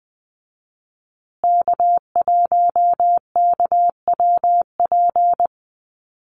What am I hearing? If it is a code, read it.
Morse code, heard as a keyed tone, about 20 wpm, 700 Hz. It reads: K1KWP
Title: USA Call Signs Repeated 3x 20wpm